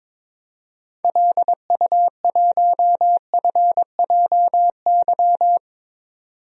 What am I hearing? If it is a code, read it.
Morse code, heard as a keyed tone, about 22 wpm, 700 Hz. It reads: LU1FJY
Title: DXCC-100 Call Signs - 6 Characters - 22wpm